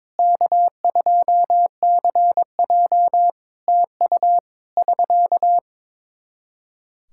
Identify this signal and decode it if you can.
Morse code, heard as a keyed tone, about 22 wpm, 700 Hz. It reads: K2CJ TU <SK>
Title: K1USN SST QSO Practice 22wpm